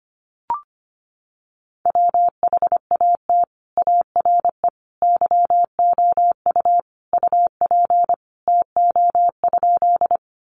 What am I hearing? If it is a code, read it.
Morse code, heard as a keyed tone, about 25 wpm, 700 Hz. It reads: WHAT ARE YOU UP TO?
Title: Sentences from Top 100 Words 25wpm